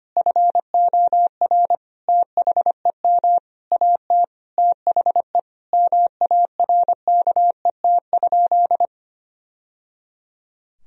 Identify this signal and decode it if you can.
Morse code, heard as a keyed tone, about 25 wpm, 700 Hz. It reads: FOR THEM AT THE MARKET?
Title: Sentences from Top 1000 Words - Review Part 3 - 25wpm